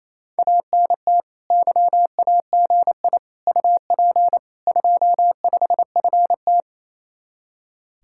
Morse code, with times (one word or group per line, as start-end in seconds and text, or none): 0.4-1.2 ANT
1.5-3.2 YAGI
3.5-4.4 UP
4.7-6.6 25FT